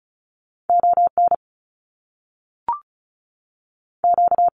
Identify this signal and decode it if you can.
Morse code, heard as a keyed tone, about 36 wpm, 700 Hz. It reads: O N Q
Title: Canadian Provinces and Territories 35wpm